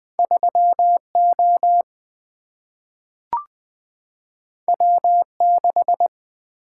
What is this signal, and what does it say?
Morse code, heard as a keyed tone, about 20 wpm, 700 Hz. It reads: 3 O W 6
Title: Single-Letter Single-Number 20wpm